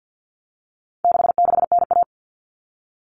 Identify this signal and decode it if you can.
Morse code, heard as a keyed tone, about 50 wpm, 700 Hz. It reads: 66NA
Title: All Cut Numbers - 4-Digit 50wpm